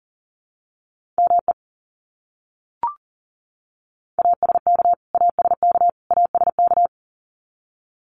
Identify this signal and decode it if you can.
Morse code, heard as a keyed tone, about 40 wpm, 700 Hz. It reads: ME ASK ASK ASK
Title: Top 200 Words Repeated 40wpm